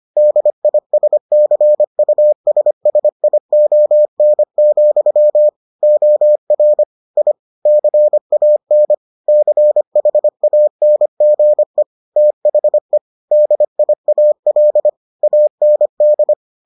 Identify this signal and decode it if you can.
Morse code, heard as a keyed tone, about 25 wpm, 600 Hz. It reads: DISCUSSION, OR I CAN CHANGE THE DIAL AND